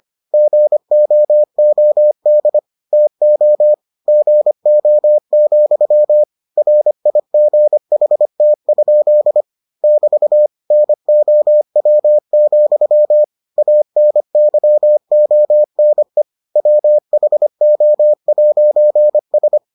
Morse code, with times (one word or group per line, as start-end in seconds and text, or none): 0.3-2.6 GOOD
2.9-3.7 TO
4.1-6.2 GO,
6.6-9.4 RIGHT?
9.8-10.5 <BT>
10.7-13.2 NOW,
13.6-16.2 ANYONE
16.6-19.6 WHO'S